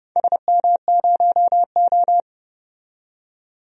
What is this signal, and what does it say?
Morse code, heard as a keyed tone, about 30 wpm, 700 Hz. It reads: SM0O
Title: DXCC-100 Call Signs - 4 Characters - 30wpm